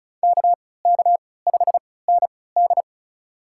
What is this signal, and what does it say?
Morse code, heard as a keyed tone, about 36 wpm, spaced as for 20 wpm, 690 Hz. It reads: KK5ND
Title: USA Call Signs - 5 Characters - 3x Character Spacing - 35wpm